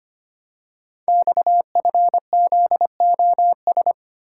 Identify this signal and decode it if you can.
Morse code, heard as a keyed tone, about 25 wpm, 700 Hz. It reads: XFZOS
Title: Code Groups - 5 Letters - 25wpm